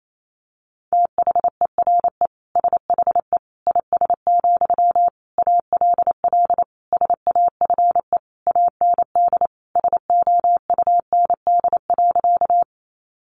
Morse code, heard as a keyed tone, about 28 wpm, 700 Hz. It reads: THERE SHE IS, ALL SAFE AND SOUND.
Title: Sentences from Top 900 Words 28wpm